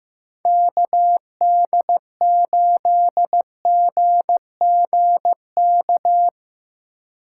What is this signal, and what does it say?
Morse code, heard as a keyed tone, about 15 wpm, 700 Hz. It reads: KD8GGK
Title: USA Call Signs - 6 Characters - 2x Character Spacing - 15wpm